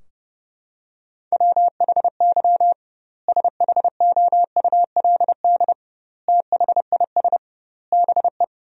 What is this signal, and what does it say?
Morse code, heard as a keyed tone, about 32 wpm, 700 Hz. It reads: WHY SHOULD THIS BE
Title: 2x Word-Spacing - Sentences from Top 300 Words - 30wpm